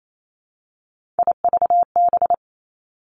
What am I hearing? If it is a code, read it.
Morse code, heard as a keyed tone, about 28 wpm, 700 Hz. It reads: IVB
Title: Three Challenging Letter-Number Combinations 28wpm